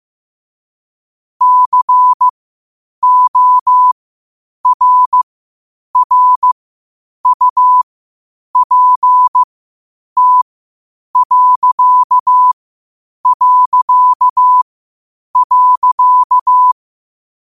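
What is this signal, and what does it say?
Morse code, heard as a keyed tone, about 15 wpm, spaced as for 8 wpm, 1000 Hz. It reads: CORRUPT...
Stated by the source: Morse code at 8 wpm.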